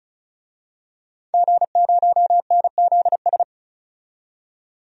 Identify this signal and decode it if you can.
Morse code, heard as a keyed tone, about 34 wpm, 700 Hz. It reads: G0NZS
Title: DXCC-100 Call Signs - 5 Characters - 35wpm